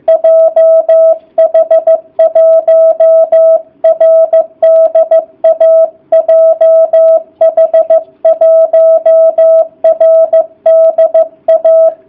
Morse code, heard as a keyed tone, about 15 wpm, 660 Hz. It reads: JH1RDA JH1RDA